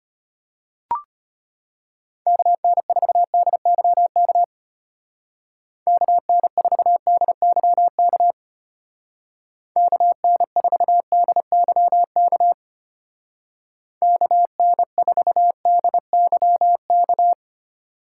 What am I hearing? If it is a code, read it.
Morse code, heard as a keyed tone, about 33 wpm, 700 Hz. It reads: KN4DYK KN4DYK KN4DYK KN4DYK